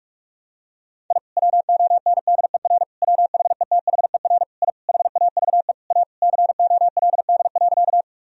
Morse code, heard as a keyed tone, about 46 wpm, 700 Hz. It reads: I WONDER WHETHER I HAVE A COLD.
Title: Sentences from Top 1000 Words - Review Part 3 - 45wpm